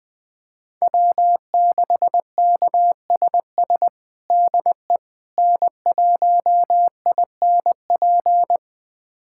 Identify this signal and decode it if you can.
Morse code, heard as a keyed tone, about 20 wpm, 700 Hz. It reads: W6KSS DE N1INP
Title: USA Callsign DE Callsign - 5 Characters - 20wpm